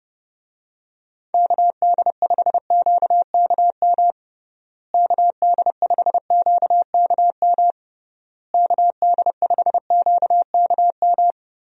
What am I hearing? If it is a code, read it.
Morse code, heard as a keyed tone, about 28 wpm, 700 Hz. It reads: KD5QKM KD5QKM KD5QKM
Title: USA Call Signs - 6 Characters - Repeated 3x - 30wpm